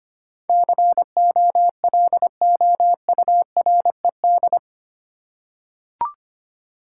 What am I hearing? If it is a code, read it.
Morse code, heard as a keyed tone, about 25 wpm, 700 Hz. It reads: COLOURED
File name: Eight Letter Words - 25wpm